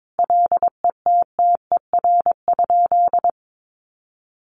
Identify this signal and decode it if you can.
Morse code, heard as a keyed tone, about 22 wpm, 700 Hz. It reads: LETTER?
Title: Sentences from Top 1300 Words 22wpm